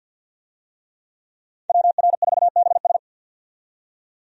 Morse code, heard as a keyed tone, about 47 wpm, 700 Hz. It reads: WR4BS
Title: USA Call Signs - 5 Characters - 50wpm